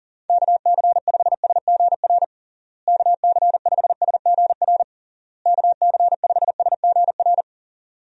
Morse code, heard as a keyed tone, about 37 wpm, 700 Hz. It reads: KC5SGR KC5SGR KC5SGR
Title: USA Call Signs - 6 Characters - Repeated 3x - 40wpm